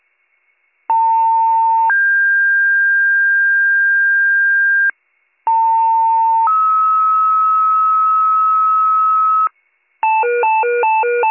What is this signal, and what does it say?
Two-tone page, heard as a one-second tone, then a three-second tone, about 900 then 1600 Hz, about 900 then 1290 Hz, with noise. It is heard from 0.9 to 9.5 s.